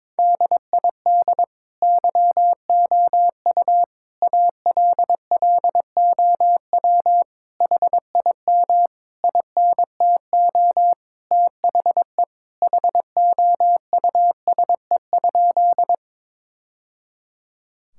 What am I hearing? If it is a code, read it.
Morse code, heard as a keyed tone, about 22 wpm, 700 Hz. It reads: DID YOU ALLOW HIM INTO THE HOUSE?
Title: Sentences from Top 1000 Words - Review Part 5 - 22wpm